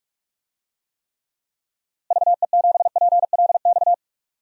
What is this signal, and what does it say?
Morse code, heard as a keyed tone, about 42 wpm, 700 Hz. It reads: VE7PLX
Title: DXCC-100 Call Signs - 6 Characters - Repeated 2x - 45wpm